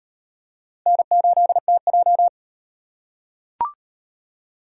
Morse code, heard as a keyed tone, about 35 wpm, 700 Hz. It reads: N8TJ